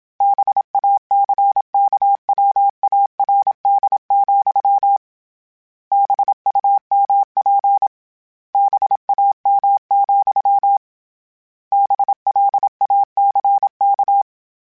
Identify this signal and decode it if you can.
Morse code, heard as a keyed tone, about 27 wpm, 800 Hz. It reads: BACKWARD, BUMP BAM, BLACK